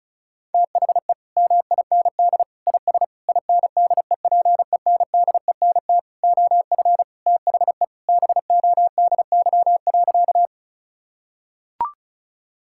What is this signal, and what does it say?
Morse code, heard as a keyed tone, about 36 wpm, 700 Hz. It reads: THE MIND IS INDEPENDENT OF THE BODY.
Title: Sentences from Top 1000 Words - Review Part 6 - 35wpm